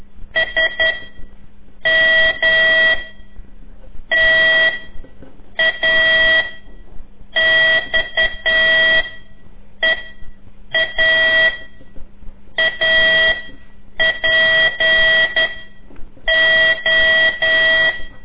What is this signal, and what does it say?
Morse code, heard as a keyed tone, about 11 wpm, 1950 Hz. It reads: S M T A X E A A P O
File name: CORSI CW